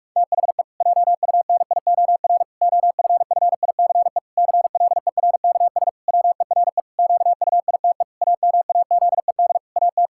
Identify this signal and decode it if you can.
Morse code, heard as a keyed tone, about 46 wpm, 700 Hz. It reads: THE JUNIOR OFFICE CLERKS WERE QUITE AMAZED AT
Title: Pangram Challenge v2 - 45wpm